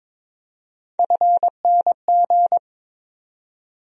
Morse code, heard as a keyed tone, about 22 wpm, 700 Hz. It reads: FNG
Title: USA Call Sign Postfixes - 3 Letters 22wpm